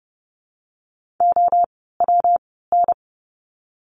Morse code, heard as a keyed tone, about 30 wpm, 700 Hz. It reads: O W N
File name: course / 30:10wpm / 115 Lesson 19f - Letter w - Words 30wpm